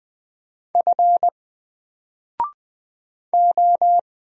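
Morse code, heard as a keyed tone, about 20 wpm, 700 Hz. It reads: F O
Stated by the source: Single letter and number Morse drill at 20 wpm, from N4LCD.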